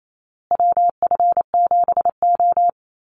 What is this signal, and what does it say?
Morse code, heard as a keyed tone, about 28 wpm, 700 Hz. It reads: WF7O
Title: USA Call Signs Repeated 2x 28wpm